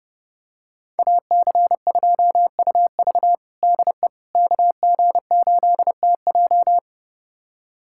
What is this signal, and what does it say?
Morse code, heard as a keyed tone, about 30 wpm, 700 Hz. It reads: AC2UV DE KG8TJ
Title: USA Callsign DE Callsign - 5 Characters - 30wpm